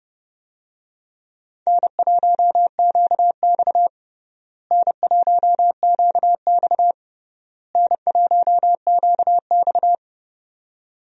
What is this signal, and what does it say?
Morse code, heard as a keyed tone, about 30 wpm, 700 Hz. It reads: N1QX N1QX N1QX
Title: USA Call Signs - 4 Characters - Repeated 3x - 30wpm